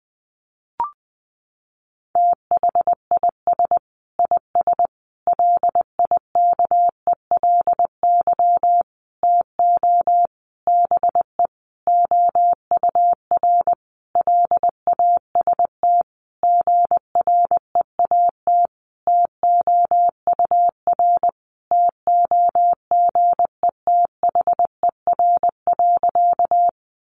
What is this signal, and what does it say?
Morse code, heard as a keyed tone, about 20 wpm, 700 Hz. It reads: THIS IS LIKELY TO BE OUR LAST GREAT TOUR TOGETHER.
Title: Sentences from Top 1400 Words 20wpm